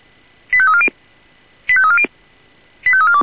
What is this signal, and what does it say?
Five-tone selective call (ZVEI), heard as quick runs of five tones, tones of about 70 ms. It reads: 85340 85340 85342